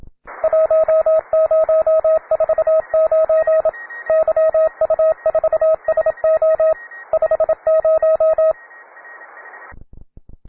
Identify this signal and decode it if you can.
Morse code, heard as a keyed tone, about 27 wpm, 640 Hz, with radio noise. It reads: 1049 YU4SO 50
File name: wae / cw beginner / ra9ac cw